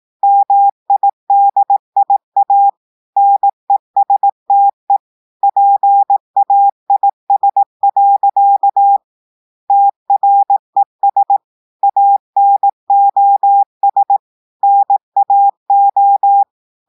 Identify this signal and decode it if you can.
Morse code, heard as a keyed tone, about 18 wpm, 800 Hz. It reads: MIDIA NESTE PAIS. TRES ANOS NAO
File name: audios / portugais / portugais texte en clair 1 vitesse 18 mots minutes